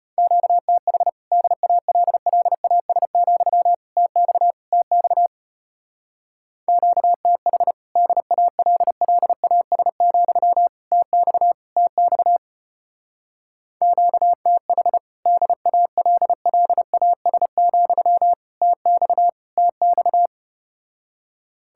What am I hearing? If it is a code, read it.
Morse code, heard as a keyed tone, about 33 wpm, 700 Hz. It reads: QTH DALLAS, TX TX QTH DALLAS, TX TX QTH DALLAS, TX TX